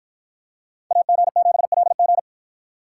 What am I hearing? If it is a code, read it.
Morse code, heard as a keyed tone, about 53 wpm, 700 Hz. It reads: AG7LG